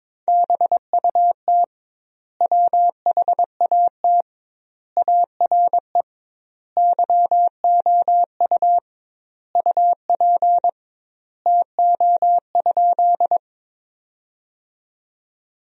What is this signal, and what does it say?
Morse code, heard as a keyed tone, about 21 wpm, 700 Hz. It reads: BUT WHAT ARE YOU UP TO?
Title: 2x Word-Spacing - Sentences from Top 100 Words - 22wpm